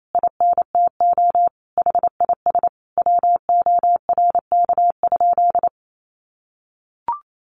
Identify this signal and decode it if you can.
Morse code, heard as a keyed tone, about 28 wpm, 700 Hz. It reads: INTO HIS WORK?
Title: Sentences from Top 1300 Words 28wpm